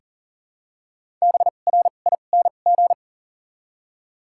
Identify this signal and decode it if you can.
Morse code, heard as a keyed tone, about 40 wpm, spaced as for 29 wpm, 700 Hz. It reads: BRING
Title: 2x Character Spacing - Top 500 Words - 40wpm